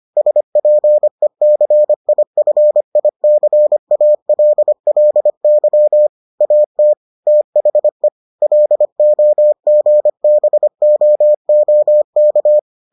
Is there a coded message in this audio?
Yes: Morse code from 0.2 to 12.6 s, heard as a keyed tone, about 25 wpm, 600 Hz. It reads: SPECIFICALLY AT THE LOGBOOK